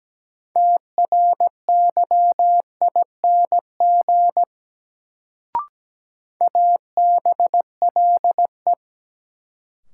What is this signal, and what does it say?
Morse code, heard as a keyed tone, about 17 wpm, 700 Hz. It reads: TRYING ABLE
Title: Sets of 2 Words - New Word Drill - 17wpm